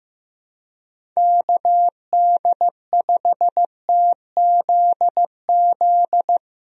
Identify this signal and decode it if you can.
Morse code, heard as a keyed tone, about 15 wpm, 700 Hz. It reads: KD5TZZ
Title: USA Call Signs - 6 Characters - Repeated 2x - 15wpm